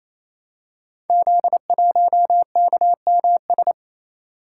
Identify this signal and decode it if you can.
Morse code, heard as a keyed tone, about 29 wpm, 700 Hz. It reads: Z1KMS
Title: Code Groups - 5 Letter:Numbers - 28wpm